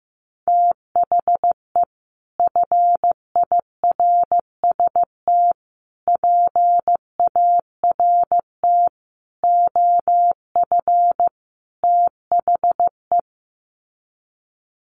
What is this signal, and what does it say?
Morse code, heard as a keyed tone, about 15 wpm, 700 Hz. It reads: THE FIRST PART OF THE
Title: Sets of 5 Words Encore 15wpm